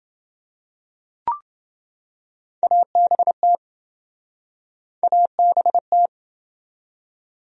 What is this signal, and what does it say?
Morse code, heard as a keyed tone, about 26 wpm, 700 Hz. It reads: ABT ABT